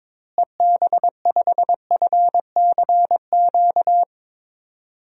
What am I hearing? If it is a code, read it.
Morse code, heard as a keyed tone, about 22 wpm, 700 Hz. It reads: EB5FCQ
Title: DXCC-100 Call Signs - 6 Characters - Repeated 2x - 22wpm